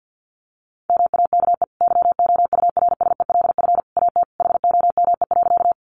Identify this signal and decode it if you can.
Morse code, heard as a keyed tone, about 48 wpm, 700 Hz. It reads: MAKE YOURSELF AT HOME.
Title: Sentences from Top 600 Words 50wpm